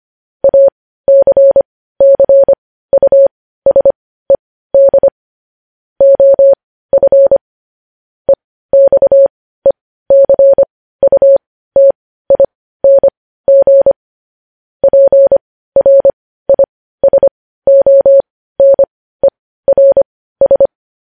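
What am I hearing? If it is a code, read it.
Morse code, heard as a keyed tone, about 25 wpm, spaced as for 15 wpm, 550 Hz. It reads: ACCUSED OF EXECUTING PRISONERS
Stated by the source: Morse code news headlines sent at 15.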